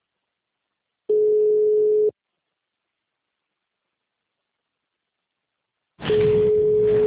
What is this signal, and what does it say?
Ringback tone (Europe, 425 Hz).